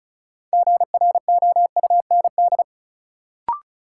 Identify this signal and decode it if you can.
Morse code, heard as a keyed tone, about 36 wpm, 700 Hz. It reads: GROUND
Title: Sets of 3 Words - New Word Drill - 35wpm